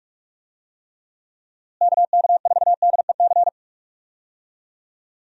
Morse code, heard as a keyed tone, about 46 wpm, 700 Hz. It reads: KK4DEC
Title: USA Call Signs - 6 Characters - Repeated 2x - 45wpm